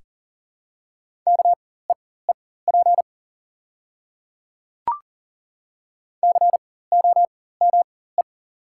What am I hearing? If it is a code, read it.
Morse code, heard as a keyed tone, about 40 wpm, spaced as for 19 wpm, 700 Hz. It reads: KEEP COME
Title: 4x Character Spacing - Top 300 Words - 40wpm